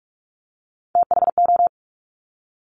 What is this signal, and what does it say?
Morse code, heard as a keyed tone, about 46 wpm, 700 Hz. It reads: THO
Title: Top Three Letter Combinations in Words 45wpm